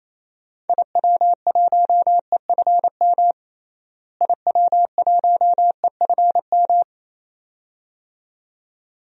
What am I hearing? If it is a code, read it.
Morse code, heard as a keyed tone, about 28 wpm, 700 Hz. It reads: IW1EFM IW1EFM
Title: DXCC-100 Call Signs - 6 Characters - Repeated 2x - 28wpm